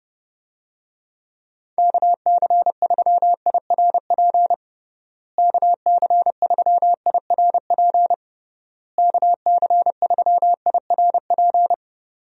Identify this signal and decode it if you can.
Morse code, heard as a keyed tone, about 29 wpm, 700 Hz. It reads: KC3IRP KC3IRP KC3IRP
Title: USA Call Signs - 6 Characters - Repeated 3x - 30wpm